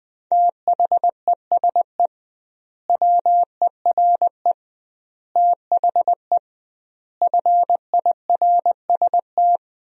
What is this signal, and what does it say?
Morse code, heard as a keyed tone, about 20 wpm, 700 Hz. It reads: THESE WERE THE FIRST
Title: 2x Word-Spacing - Sentences from Top 300 Words - 20wpm